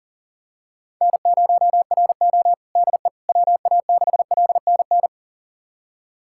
Morse code, heard as a keyed tone, about 37 wpm, 700 Hz. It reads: N0RO DE WA6LNN
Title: USA Callsign DE Callsign 40wpm